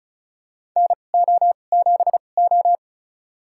Morse code, heard as a keyed tone, about 34 wpm, spaced as for 25 wpm, 700 Hz. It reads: NO7O
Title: USA Call Signs - 4 Characters - 2x Character Spacing - 35wpm